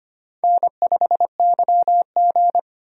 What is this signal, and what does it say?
Morse code, heard as a keyed tone, about 25 wpm, 700 Hz. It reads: N5YG
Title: USA Call Signs - 4 Characters - 25wpm